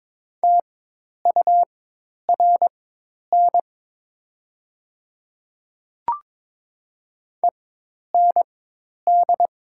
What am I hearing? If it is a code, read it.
Morse code, heard as a keyed tone, about 22 wpm, spaced as for 10 wpm, 690 Hz. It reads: TURN END
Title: 4x Character Spacing - Top 200 Words - 22wpm